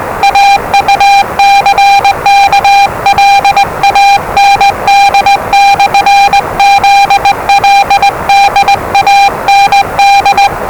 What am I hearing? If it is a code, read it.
Morse code, heard as a keyed tone, about 20 wpm, 800 Hz, with noise. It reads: AUCKLAND/ZLD AND
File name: WellingtonRadioZlwCloseDownOnHfAndMf / 01-Zlw 4277khz 417.5 qrt until 0925z